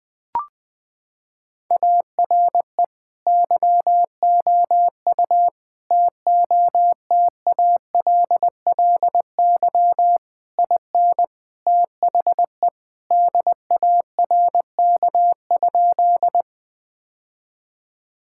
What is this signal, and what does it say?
Morse code, heard as a keyed tone, about 20 wpm, 700 Hz. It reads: ARE YOU TOTALLY IN THE DARK?
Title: Sentences from Top 1600 Words 20wpm